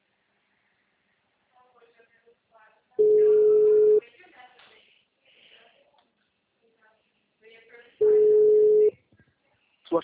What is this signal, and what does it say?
Ringback tone (Europe, 425 Hz), with speech.